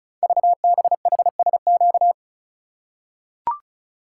Morse code, heard as a keyed tone, about 36 wpm, 700 Hz. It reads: VBHSQ
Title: Code Groups - 5 Letter:Numbers - 35wpm